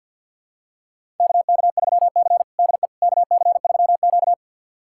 Morse code, heard as a keyed tone, about 51 wpm, 700 Hz. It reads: KK3C DE KC3Q
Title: USA Callsign DE Callsign - 4 Characters - 50wpm